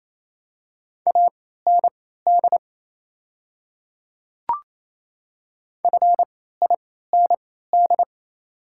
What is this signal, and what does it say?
Morse code, heard as a keyed tone, about 28 wpm, spaced as for 16 wpm, 700 Hz. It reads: AND FIND
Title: 3x Character Spacing - Top 200 Words - 28wpm